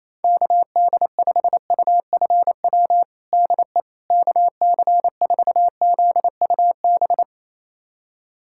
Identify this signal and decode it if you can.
Morse code, heard as a keyed tone, about 28 wpm, 700 Hz. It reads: KD5UFW DE KC4ZUB
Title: USA Callsign DE Callsign - 6 Characters - 28wpm